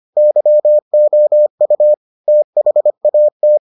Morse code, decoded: YOU THAT

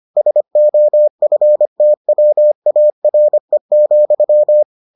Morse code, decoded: SOFTWARE,

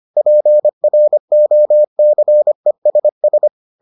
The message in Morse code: PROCESS

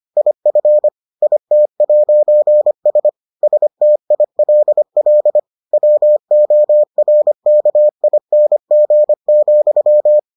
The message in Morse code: IF IT'S STILL WORKING,

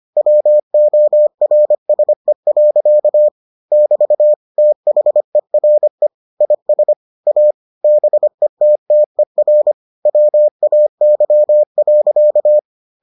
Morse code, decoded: WORSE. <BT> THERE IS A BETTER WAY.

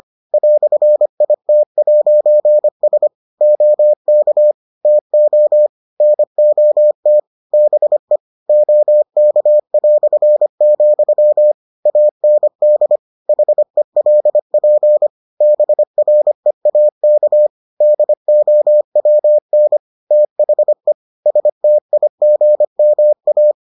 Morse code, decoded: "IT'S OK TO NOT BE OK", AND HELP BREAK DOWN THE STIGMA

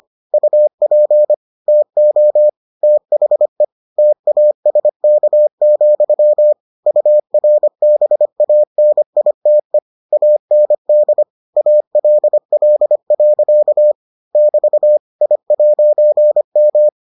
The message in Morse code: UP TO THE TASK, URBANITE AND ALL. <BT> I'M

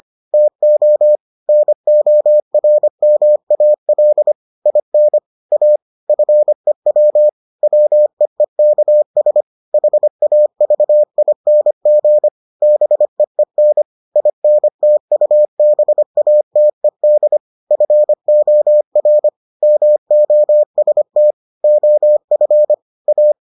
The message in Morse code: TO NORMAL IN A FEW WEEKS HAVING BEEN INTUBATED FOR MOST OF A

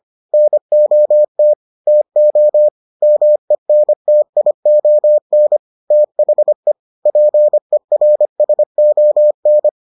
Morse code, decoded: NOT TO MENTION THE PERSON

